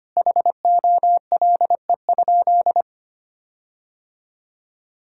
Morse code, heard as a keyed tone, about 25 wpm, 700 Hz. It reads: HOLE?
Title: Sentences from Top 1600 Words 25wpm